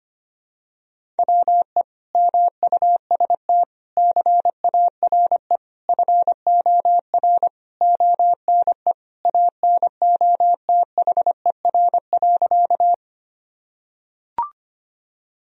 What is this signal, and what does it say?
Morse code, heard as a keyed tone, about 25 wpm, 700 Hz. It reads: WE MUST CARE FOR ONE ANOTHER.
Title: Sentences from Top 1000 Words - Review Part 1 - 25wpm